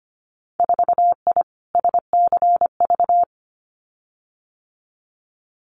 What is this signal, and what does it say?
Morse code, heard as a keyed tone, about 25 wpm, 700 Hz. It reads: 4I SCV
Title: Winter Field Day Exchanges Encore 25wpm